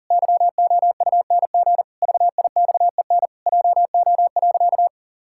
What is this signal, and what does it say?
Morse code, heard as a keyed tone, about 40 wpm, 700 Hz. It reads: YOUNG VIXEN JO.
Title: Pangram Challenge v2 - 40wpm